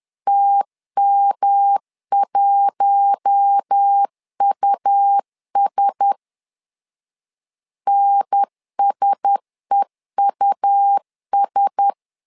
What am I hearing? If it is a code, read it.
Morse code, heard as a keyed tone, about 10 wpm, 800 Hz. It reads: TM1US NSEUS
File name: LEZIONI / 4 Q1S 10wpm 2133